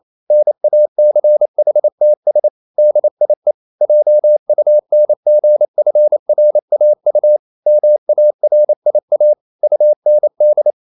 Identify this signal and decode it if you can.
Morse code, heard as a keyed tone, about 28 wpm, 600 Hz. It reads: NACHTS DIE JUNGFRAU MARIA UND
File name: marienkind9 28wpm